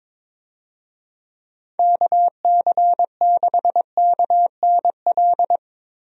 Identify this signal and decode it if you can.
Morse code, heard as a keyed tone, about 22 wpm, 700 Hz. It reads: KC6KNL
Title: USA Call Signs - 6 Characters - 22wpm